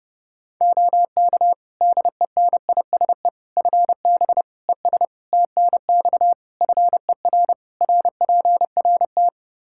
Morse code, heard as a keyed tone, about 30 wpm, 700 Hz. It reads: OK DENISE FB ES TNX FER RPRT
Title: Practice QSO - Exchange 2 - Info - 30wpm